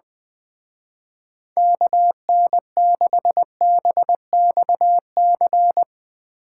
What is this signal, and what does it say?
Morse code, heard as a keyed tone, about 21 wpm, 700 Hz. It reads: KN6BXC
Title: Speed Racing - USA 2x3 Call Signs 20wpm